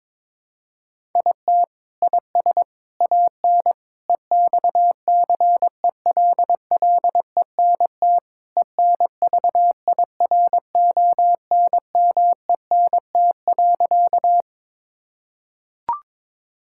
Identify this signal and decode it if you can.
Morse code, heard as a keyed tone, about 22 wpm, 700 Hz. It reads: IT IS AN EXCELLENT ENVIRONMENT.
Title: Sentences from Top 1000 Words - Review Part 5 - 22wpm